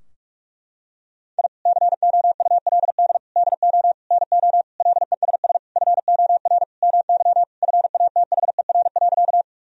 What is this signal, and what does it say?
Morse code, heard as a keyed tone, about 41 wpm, 700 Hz. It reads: I COULD DO NO LESS FOR MY FATHER.
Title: Sentences from Top 1000 Words - Review Part 2 - 45wpm